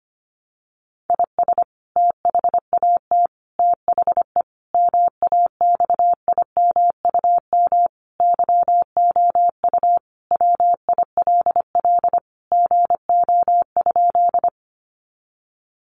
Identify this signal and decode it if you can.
Morse code, heard as a keyed tone, about 25 wpm, 700 Hz. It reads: IS THAT THE MAXIMUM YOU WILL GO?
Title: Sentences from Top 1300 Words 25wpm